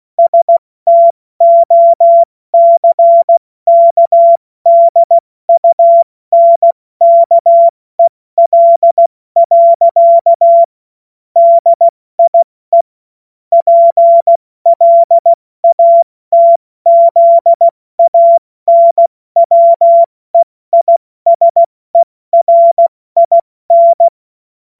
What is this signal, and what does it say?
Morse code, heard as a keyed tone, about 16 wpm, 680 Hz. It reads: STOCKDUNKEL. DIE PLATZANWEISERIN
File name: Morsekurs / Witze / Witz 13 Mai 2025 70 F80